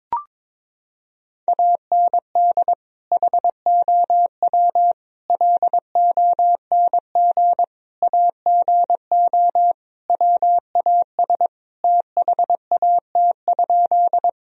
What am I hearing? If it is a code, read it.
Morse code, heard as a keyed tone, about 22 wpm, 700 Hz. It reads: AND HOW LONG AGO WAS THAT?
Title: Sentences from Top 1000 Words - Review Part 3 - 22wpm